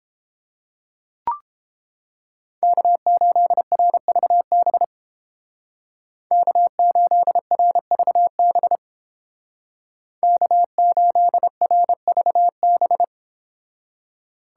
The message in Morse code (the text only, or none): K8RVB K8RVB K8RVB